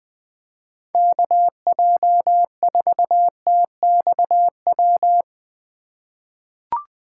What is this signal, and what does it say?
Morse code, heard as a keyed tone, about 20 wpm, 700 Hz. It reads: KJ4TXW
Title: USA Call Signs - 6 Characters - 4x Character Spacing - 20wpm